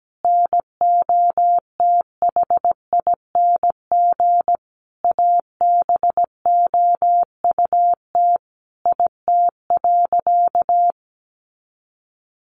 Morse code, heard as a keyed tone, about 17 wpm, 700 Hz. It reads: NOTHING ABOUT IT.
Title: Sentences from Top 500 Words 17wpm